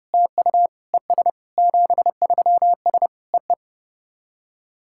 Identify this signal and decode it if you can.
Morse code, heard as a keyed tone, about 30 wpm, 690 Hz. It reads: TU ES 73S EE
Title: SOTA QSO Practice 30wpm